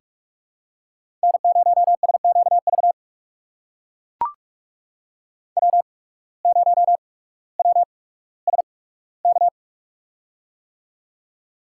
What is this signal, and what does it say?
Morse code, heard as a keyed tone, about 44 wpm, spaced as for 19 wpm, 700 Hz. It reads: N0SQV W0WSK
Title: USA Call Signs - 5 Characters - 8x Character Spacing - 45wpm